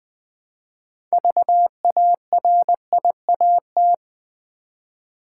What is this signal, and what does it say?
Morse code, heard as a keyed tone, about 20 wpm, 700 Hz. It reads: VARIAT
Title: Build A Word - 3 Letters At A Time - 9-Letter Words - 20wpm